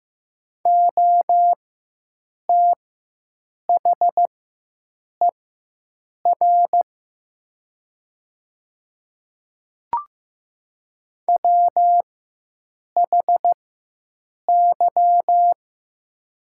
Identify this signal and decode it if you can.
Morse code, heard as a keyed tone, about 15 wpm, spaced as for 7 wpm, 700 Hz. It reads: OTHER WHY
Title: 4x Character Spacing - Top 200 Words - 15wpm